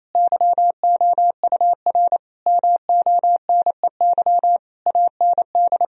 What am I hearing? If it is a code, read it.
Morse code, heard as a keyed tone, about 28 wpm, 700 Hz. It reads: YOUR MONEY AND
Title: Sentences from Top 1000 Words - Review Part 4 - 28wpm